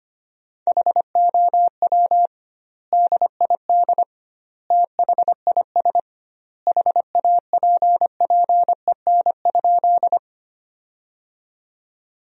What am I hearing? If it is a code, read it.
Morse code, heard as a keyed tone, about 25 wpm, 700 Hz. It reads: HOW DID THIS HAPPEN?
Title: 2x Word-Spacing - Sentences from Top 400 Words - 25wpm